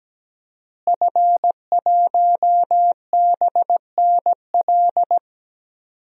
Morse code, heard as a keyed tone, about 17 wpm, 700 Hz. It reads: F1BNL
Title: DXCC-100 Call Signs - 5 Characters - Repeated 2x - 17wpm